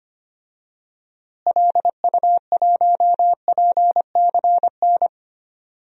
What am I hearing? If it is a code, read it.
Morse code, heard as a keyed tone, about 25 wpm, 700 Hz. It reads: LU1PCN
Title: DXCC-100 Call Signs - 6 Characters - Repeated 3x - 25wpm